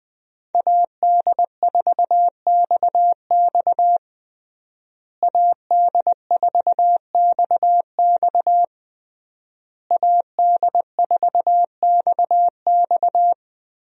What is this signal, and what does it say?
Morse code, heard as a keyed tone, about 20 wpm, 700 Hz. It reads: AD4XX AD4XX AD4XX
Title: USA Call Signs - 5 Characters - Repeated 3x - 20wpm